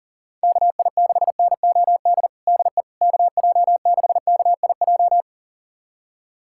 Morse code, heard as a keyed tone, about 40 wpm, 700 Hz. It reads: KI6NOD DE KJ6KIJ
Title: USA Callsign DE Callsign - 6 Characters - 40wpm